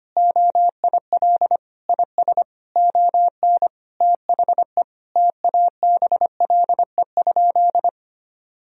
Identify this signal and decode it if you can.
Morse code, heard as a keyed tone, about 25 wpm, 700 Hz. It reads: OIL IS ON THE TABLE?
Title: Sentences from Top 1000 Words - Review Part 4 - 25wpm